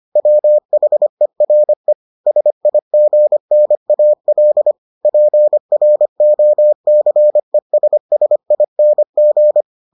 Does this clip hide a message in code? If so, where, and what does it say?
Yes: Morse code from 0.2 to 9.6 s, heard as a keyed tone, about 25 wpm, 600 Hz. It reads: WHERE SIGNAL PROCESSING